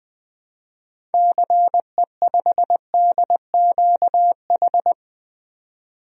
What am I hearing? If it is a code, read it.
Morse code, heard as a keyed tone, about 20 wpm, 700 Hz. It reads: CE5DQH
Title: DXCC-100 Call Signs - 6 Characters - 20wpm